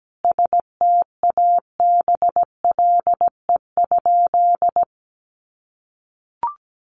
Morse code, heard as a keyed tone, about 17 wpm, 700 Hz. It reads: STABLE?
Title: Sentences from Top 1500 Words 17wpm